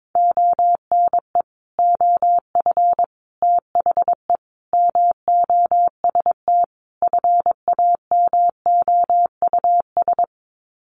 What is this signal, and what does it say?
Morse code, heard as a keyed tone, about 22 wpm, 700 Hz. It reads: ONE OF THE MOST FAMOUS
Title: Sets of 5 Words Encore2 22wpm